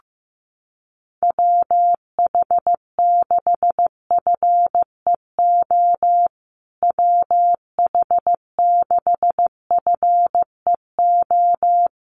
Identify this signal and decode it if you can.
Morse code, heard as a keyed tone, about 14 wpm, 700 Hz. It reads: WH6FEO WH6FEO
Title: USA Call Signs Repeated 3x 15wpm